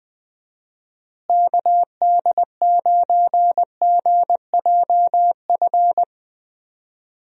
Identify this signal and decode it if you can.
Morse code, heard as a keyed tone, about 20 wpm, 700 Hz. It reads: KD9GJF
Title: USA Call Signs - 6 Characters - Repeated 2x - 20wpm